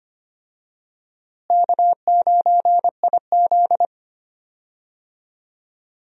Morse code, heard as a keyed tone, about 25 wpm, 700 Hz. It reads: K9IZ
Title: USA Call Signs - 4 Characters - Repeated 2x - 25wpm